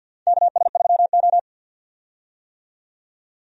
Morse code, heard as a keyed tone, about 49 wpm, 700 Hz. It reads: KS3O
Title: USA Call Signs - 4 Characters - 50wpm